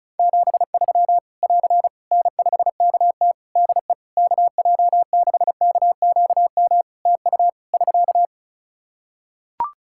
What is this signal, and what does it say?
Morse code, heard as a keyed tone, about 34 wpm, 700 Hz. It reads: 73 <AR> N5KT DE KJ6KQM TU <SK>
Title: Practice QSO - Exchange 4 - Ending - 35wpm